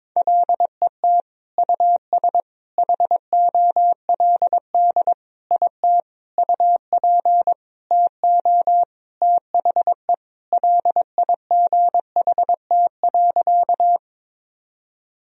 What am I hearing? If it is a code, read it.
Morse code, heard as a keyed tone, about 22 wpm, 700 Hz. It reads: LET US HOLD IT UP TO THE LIGHT.
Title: Sentences from Top 1000 Words - Review Part 2 - 22wpm